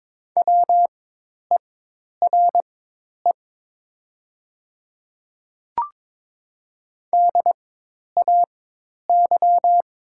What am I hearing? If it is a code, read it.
Morse code, heard as a keyed tone, about 22 wpm, spaced as for 10 wpm, 700 Hz. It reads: WERE DAY
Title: 4x Character Spacing - Top 100 Words - 22wpm